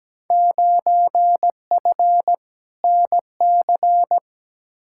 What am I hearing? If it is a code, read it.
Morse code, heard as a keyed tone, about 17 wpm, 700 Hz. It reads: 9F NC
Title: Field Day Exchanges Encore - 17wpm